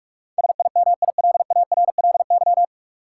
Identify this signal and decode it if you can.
Morse code, heard as a keyed tone, about 46 wpm, 700 Hz. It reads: SIMILARLY
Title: Sentences from Top 1400 Words - New Word Drill Alt - 45wpm